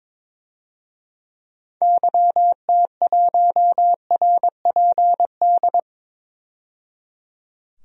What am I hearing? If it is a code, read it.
Morse code, heard as a keyed tone, about 22 wpm, 700 Hz. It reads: YT1RPD
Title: DXCC-100 Call Signs - 6 Characters - 22wpm